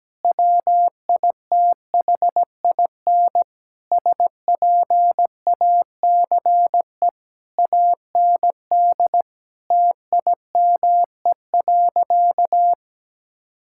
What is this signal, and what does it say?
Morse code, heard as a keyed tone, about 17 wpm, 700 Hz. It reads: WITHIN SPACE AND TIME.
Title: Sentences from Top 1000 Words - Review Part 5 - 17wpm